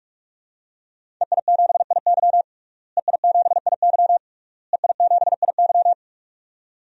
Morse code, heard as a keyed tone, about 46 wpm, 700 Hz. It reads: EI7IY EI7IY EI7IY
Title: DXCC-100 Call Signs - 5 Characters - Repeated 3x - 45wpm